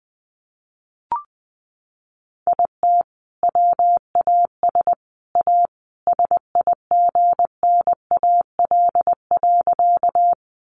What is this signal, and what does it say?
Morse code, heard as a keyed tone, about 20 wpm, 700 Hz. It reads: IT WAS A SIGNAL.
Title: Sentences from Top 1400 Words 20wpm